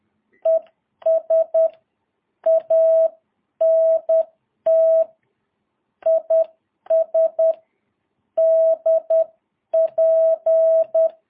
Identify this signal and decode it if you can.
Morse code, heard as a keyed tone, about 10 wpm, 660 Hz. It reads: ES ANT IS DP
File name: jh8cbh / 2022cw